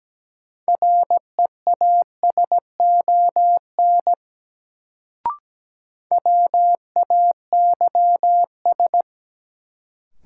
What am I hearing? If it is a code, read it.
Morse code, heard as a keyed tone, about 17 wpm, 700 Hz. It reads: REASON WAYS
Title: Sets of 5 Words Encore - New Word Drill - 17wpm